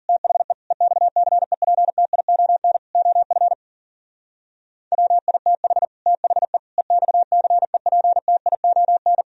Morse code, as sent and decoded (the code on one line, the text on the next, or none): - .... / . -..- -.-. .--. - .. --- -. / --- ..-. / .-- .. - .... / - .... / . -..- -.-. .--. - .. --- -.
THE EXCEPTION OF WITH THE EXCEPTION